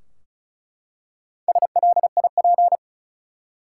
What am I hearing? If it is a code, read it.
Morse code, heard as a keyed tone, about 36 wpm, 700 Hz. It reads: SLIP
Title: Four Letter Words - 35wpm